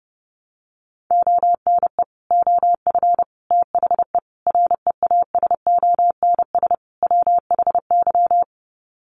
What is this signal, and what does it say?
Morse code, heard as a keyed tone, about 30 wpm, 700 Hz. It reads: ONE OF THE REASONS WHY
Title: Sets of 5 Words 30wpm